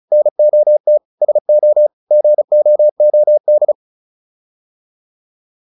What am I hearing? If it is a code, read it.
Morse code, heard as a keyed tone, about 34 wpm, 600 Hz. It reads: NOT SO GOOD